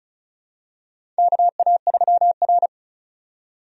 Morse code, heard as a keyed tone, about 34 wpm, 700 Hz. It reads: KA3R
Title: USA Call Signs - 4 Characters - 35wpm